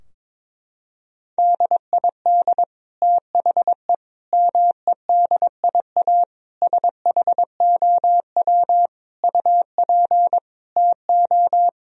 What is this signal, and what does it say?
Morse code, heard as a keyed tone, about 22 wpm, 700 Hz. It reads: DID THE MEDIA SHOW UP TO
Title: Sentences from Top 1000 Words - Review Part 4 - 22wpm